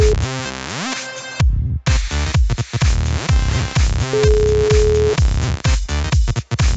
Ringback tone (Europe, 425 Hz), with noise.